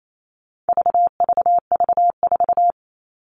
Morse code, heard as a keyed tone, about 28 wpm, 700 Hz. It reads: VVV4
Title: All Cut Numbers - 4-Digit 28wpm